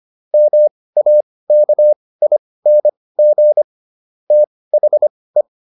Morse code, heard as a keyed tone, about 25 wpm, spaced as for 18 wpm, 600 Hz. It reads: MAKING THE